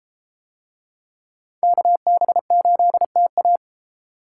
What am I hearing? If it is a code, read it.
Morse code, heard as a keyed tone, about 30 wpm, 700 Hz. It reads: KB8TA